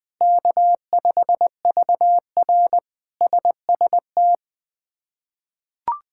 Morse code, heard as a keyed tone, about 21 wpm, 700 Hz. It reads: K5VR SST
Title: K1USN SST QSO Practice - 20wpm